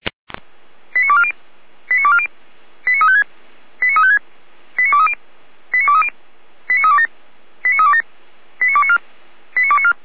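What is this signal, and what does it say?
Five-tone selective call (ZVEI), heard as quick runs of five tones, tones of about 70 ms. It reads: 89240 89240 89356 89356 89230 89230 89327 89327 89294 89294
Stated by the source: ZVEI five-tone call received by SDR.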